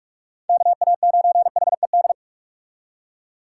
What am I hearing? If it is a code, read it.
Morse code, heard as a keyed tone, about 56 wpm, 700 Hz. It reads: KA9HED